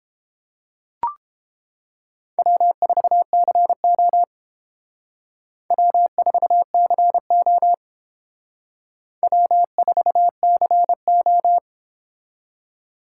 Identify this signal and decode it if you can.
Morse code, heard as a keyed tone, about 29 wpm, 700 Hz. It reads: W4CO W4CO W4CO